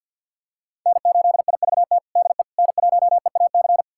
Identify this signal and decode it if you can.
Morse code, heard as a keyed tone, about 52 wpm, 700 Hz. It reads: N8IVT DE N1EAC
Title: USA Callsign DE Callsign - 5 Characters - 50wpm